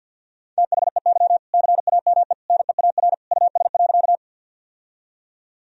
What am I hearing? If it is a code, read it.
Morse code, heard as a keyed tone, about 46 wpm, 700 Hz. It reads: THEY CAME NEAR US.
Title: Sentences from Top 1000 Words - Review Part 1 - 50wpm